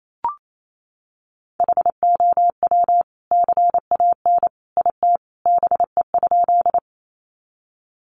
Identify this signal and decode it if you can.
Morse code, heard as a keyed tone, about 28 wpm, 700 Hz. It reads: HOW CAN IT BE?
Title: Sentences from Top 100 Words 28wpm